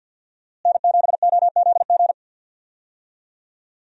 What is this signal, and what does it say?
Morse code, heard as a keyed tone, about 52 wpm, 700 Hz. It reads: N7OZG
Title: USA Call Signs - 5 Characters - Repeated 3x - 50wpm